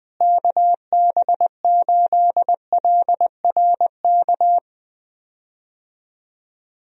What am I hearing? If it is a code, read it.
Morse code, heard as a keyed tone, about 20 wpm, 700 Hz. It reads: KB8LRK
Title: USA Call Signs - 6 Characters - Repeated 3x - 20wpm